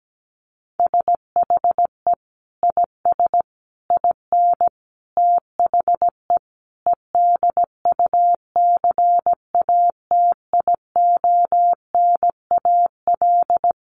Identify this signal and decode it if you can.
Morse code, heard as a keyed tone, about 17 wpm, 700 Hz. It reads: SHE IS IN THE EDUCATIONAL
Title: Sentences from Top 1200 Words 17wpm